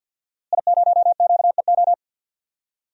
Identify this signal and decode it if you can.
Morse code, heard as a keyed tone, about 49 wpm, 700 Hz. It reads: I0QEO
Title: DXCC-100 Call Signs - 5 Characters - Repeated 3x - 50wpm